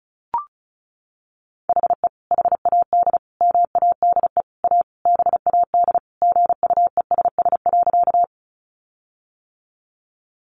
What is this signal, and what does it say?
Morse code, heard as a keyed tone, about 34 wpm, 700 Hz. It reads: HE HAD MADE A BAD GUESS.
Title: Sentences from Top 1500 Words 35wpm